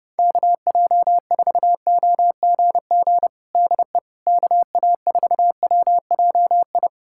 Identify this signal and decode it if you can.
Morse code, heard as a keyed tone, about 30 wpm, 700 Hz. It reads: KJ4OGG DE KA4WJI
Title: USA Callsign DE Callsign - 6 Characters - 30wpm